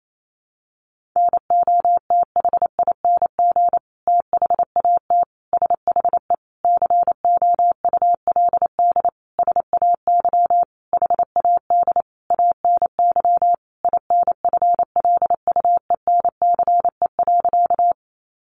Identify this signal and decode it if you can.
Morse code, heard as a keyed tone, about 28 wpm, 700 Hz. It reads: NOTHING THAT SHE COULD SAY HAD ANY INFLUENCE.
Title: Sentences from Top 1100 Words 28wpm